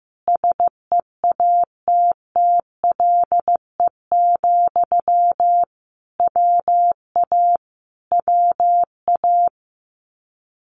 Morse code, heard as a keyed tone, about 15 wpm, 700 Hz. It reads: SEATTLE, WA WA
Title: Common QSO Phrases Encore 15wpm